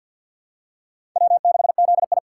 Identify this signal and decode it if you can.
Morse code, heard as a keyed tone, about 47 wpm, 700 Hz. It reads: W6ZI
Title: USA Call Signs - 4 Characters - 2x Character Spacing - 50wpm